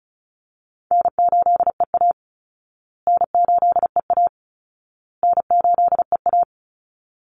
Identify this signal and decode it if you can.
Morse code, heard as a keyed tone, about 34 wpm, 700 Hz. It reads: N8EA N8EA N8EA
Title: Call Signs Repeated 2x - 35wpm